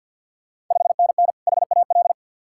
Morse code, heard as a keyed tone, about 48 wpm, 690 Hz. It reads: 5NN HAL
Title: Ontario QSO Party Exchange 50wpm